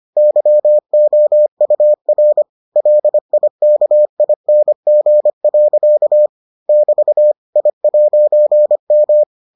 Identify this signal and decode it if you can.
Morse code, heard as a keyed tone, about 25 wpm, 600 Hz. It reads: YOUR LIKING. <BT> I'M